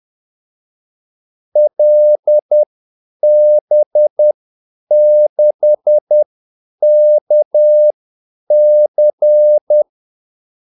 Morse code, heard as a keyed tone, about 10 wpm, spaced as for 8 wpm, 600 Hz. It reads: LB6KC
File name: Callsigns-10-8wpm